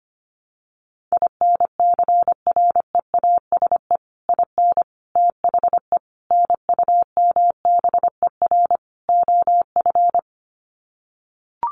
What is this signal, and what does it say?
Morse code, heard as a keyed tone, about 25 wpm, 700 Hz. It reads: INCREASE IN THE NUMBER OF
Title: Sets of 5 Words 25wpm